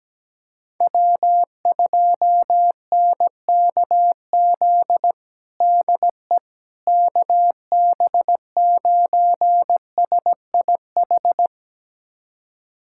Morse code, heard as a keyed tone, about 17 wpm, 700 Hz. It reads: W2NKZ DE KB9SIH
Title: USA Callsign DE Callsign 17wpm